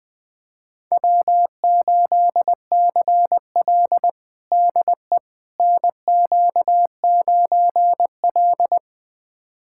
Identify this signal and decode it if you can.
Morse code, heard as a keyed tone, about 20 wpm, 700 Hz. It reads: W8CL DE NQ9L
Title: USA Callsign DE Callsign - 4 Characters - 20wpm